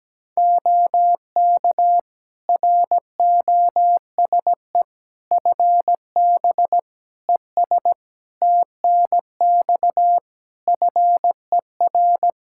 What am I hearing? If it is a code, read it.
Morse code, heard as a keyed tone, about 17 wpm, 700 Hz. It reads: OK ROSE FB ES TNX FER
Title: Practice QSO - Exchange 2 - Info - 17wpm